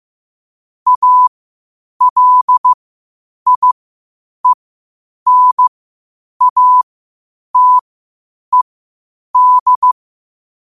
Morse code, heard as a keyed tone, about 15 wpm, spaced as for 8 wpm, 1000 Hz. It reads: ALIENATED
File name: qotdfiles / 8wpm